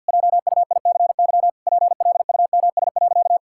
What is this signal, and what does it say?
Morse code, heard as a keyed tone, about 50 wpm, 700 Hz. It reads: JUICY PLUMS.
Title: Pangram Challenge v2 - 50wpm